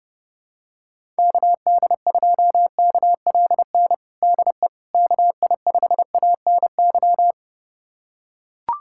Morse code, heard as a keyed tone, about 30 wpm, 700 Hz. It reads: KD2KLN DE KI5ANY
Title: USA Callsign DE Callsign - 6 Characters - 30wpm